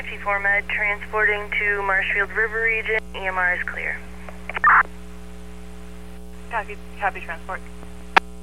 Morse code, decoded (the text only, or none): none